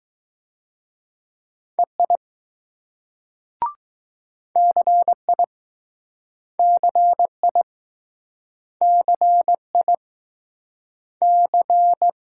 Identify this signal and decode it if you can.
Morse code, heard as a keyed tone, about 21 wpm, 700 Hz. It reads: EI CI CI CI C